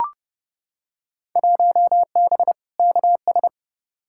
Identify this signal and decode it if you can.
Morse code, heard as a keyed tone, about 30 wpm, 700 Hz. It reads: 1B KS
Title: Field Day Exchanges Encore - 30wpm